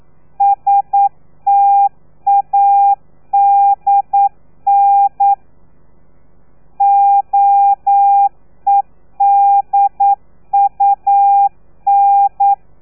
Morse code, heard as a keyed tone, about 9 wpm, 800 Hz. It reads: STADN OEDUN
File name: telegrafia / i2tfj / Lez3 40